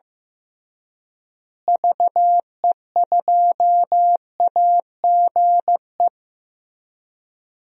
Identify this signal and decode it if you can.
Morse code, heard as a keyed tone, about 15 wpm, 700 Hz. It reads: VE2AGE
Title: DXCC-100 Call Signs - 6 Characters - Repeated 3x - 15wpm